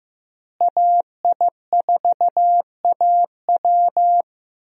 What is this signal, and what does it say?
Morse code, heard as a keyed tone, about 16 wpm, 700 Hz. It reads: AI4AW
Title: USA Call Signs - 5 Characters - Repeated 3x - 15wpm